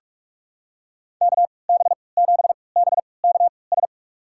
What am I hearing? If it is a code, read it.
Morse code, heard as a keyed tone, about 46 wpm, spaced as for 25 wpm, 700 Hz. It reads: KB7BKS
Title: USA Call Signs - 6 Characters - 3x Character Spacing - 45wpm